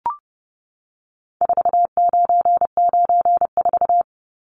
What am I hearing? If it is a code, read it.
Morse code, heard as a keyed tone, about 30 wpm, 700 Hz. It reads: 4994
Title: Numbers - 4-Digit 30wpm